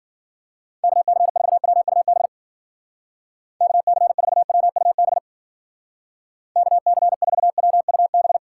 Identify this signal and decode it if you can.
Morse code, heard as a keyed tone, about 51 wpm, 700 Hz. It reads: KC4WUB KC4WUB KC4WUB